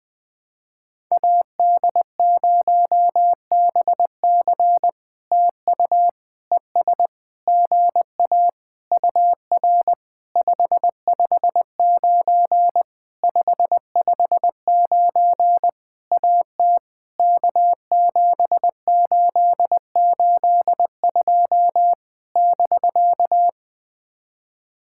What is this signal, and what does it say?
Morse code, heard as a keyed tone, about 20 wpm, 700 Hz. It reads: AD0BC TU ES GA UR 559 559 AT K7882 <BK>
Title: POTA QSO Practice - 20wpm